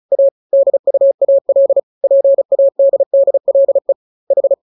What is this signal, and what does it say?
Morse code, heard as a keyed tone, about 35 wpm, 550 Hz. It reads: A DUAL PADDLE H
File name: jokes / joke 1 35 0000